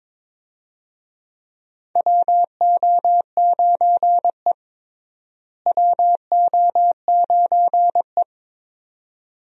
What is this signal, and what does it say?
Morse code, heard as a keyed tone, about 22 wpm, 700 Hz. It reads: WO9E WO9E
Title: USA Call Signs - 4 Characters - Repeated 2x - 22wpm